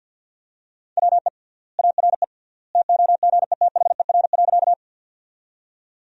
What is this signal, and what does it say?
Morse code, heard as a keyed tone, about 50 wpm, 700 Hz. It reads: WE ARE TOGETHER.
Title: 3x Word-Spacing - Sentences from Top 300 Words - 50wpm